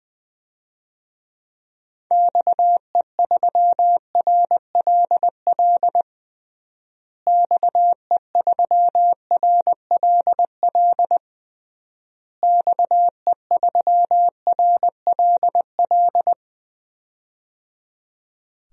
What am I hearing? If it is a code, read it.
Morse code, heard as a keyed tone, about 20 wpm, 700 Hz. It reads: XE3RLL XE3RLL XE3RLL
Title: DXCC-100 Call Signs - 6 Characters - Repeated 3x - 20wpm